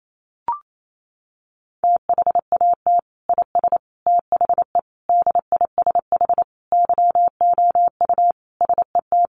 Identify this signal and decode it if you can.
Morse code, heard as a keyed tone, about 28 wpm, 700 Hz. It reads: THAT IS THE DISH YOU SET
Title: Sentences from Top 1300 Words 28wpm